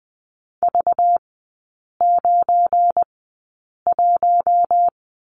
Morse code, heard as a keyed tone, about 20 wpm, spaced as for 8 wpm, 700 Hz. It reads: V91
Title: 191 Lesson 31e - Letter v - Three Characters 20wpm